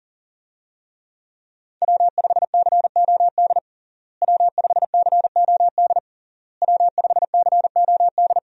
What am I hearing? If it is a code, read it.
Morse code, heard as a keyed tone, about 40 wpm, 700 Hz. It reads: W5COD W5COD W5COD
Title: USA Call Signs - 5 Characters - Repeated 3x - 40wpm